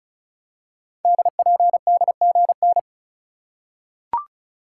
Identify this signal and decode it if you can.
Morse code, heard as a keyed tone, about 36 wpm, 700 Hz. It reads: DPDGN
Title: Code Groups - 5 Letters - 35wpm